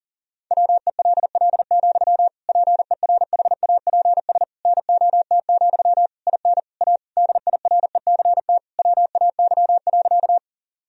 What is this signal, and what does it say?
Morse code, heard as a keyed tone, about 37 wpm, 700 Hz. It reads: WELL, PERHAPS NOT, IN A DIRECT WAY.
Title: Sentences from Top 1000 Words - Review Part 2 - 40wpm